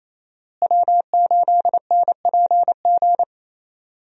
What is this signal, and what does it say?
Morse code, heard as a keyed tone, about 28 wpm, 700 Hz. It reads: W8NPG
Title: USA Call Signs - 5 Characters - Repeated 3x - 28wpm